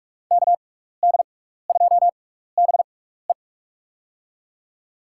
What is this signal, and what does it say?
Morse code, heard as a keyed tone, about 46 wpm, spaced as for 16 wpm, 700 Hz. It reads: KD2BE
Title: USA Call Signs - 6x Character Spacing - 45wpm